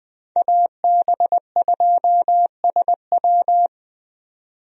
Morse code, heard as a keyed tone, about 20 wpm, 700 Hz. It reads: AB2SW
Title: USA Call Signs - 5 Characters - 20wpm